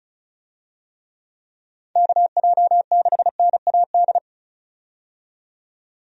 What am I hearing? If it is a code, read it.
Morse code, heard as a keyed tone, about 34 wpm, 700 Hz. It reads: KJ6NAD
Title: USA Call Signs - 6 Characters - 35wpm